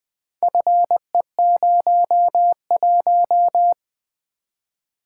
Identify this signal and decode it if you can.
Morse code, heard as a keyed tone, about 20 wpm, 700 Hz. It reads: FE01
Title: Four Letter-Number Combinations - 20wpm